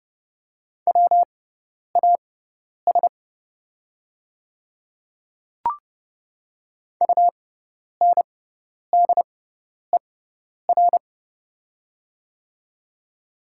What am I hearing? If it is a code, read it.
Morse code, heard as a keyed tone, about 30 wpm, spaced as for 10 wpm, 700 Hz. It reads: WAS UNDER
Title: 6x Character Spacing - Top 200 Words - 30wpm